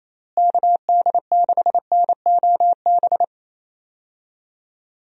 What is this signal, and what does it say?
Morse code, heard as a keyed tone, about 27 wpm, 700 Hz. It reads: KD6NOB
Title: USA Call Signs - 6 Characters - Repeated 3x - 28wpm